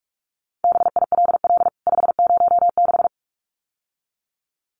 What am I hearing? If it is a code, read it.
Morse code, heard as a keyed tone, about 44 wpm, 700 Hz. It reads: BILL 506
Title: CWT Exchange 45wpm